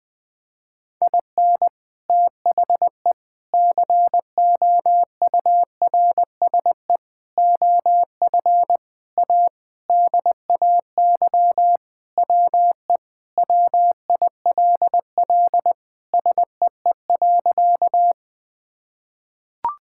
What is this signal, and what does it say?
Morse code, heard as a keyed tone, about 20 wpm, 700 Hz. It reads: IN THE COURSE OF A DAY WE WILL SEE.
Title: Sentences with 3-Word Phrases - 20wpm